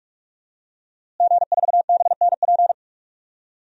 Morse code, heard as a keyed tone, about 44 wpm, 700 Hz. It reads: G4BNP
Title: DXCC-100 Call Signs - 5 Characters - Repeated 3x - 45wpm